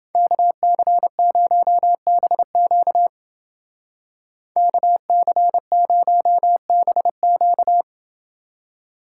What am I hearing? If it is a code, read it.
Morse code, heard as a keyed tone, about 29 wpm, 700 Hz. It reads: KC0BQ KC0BQ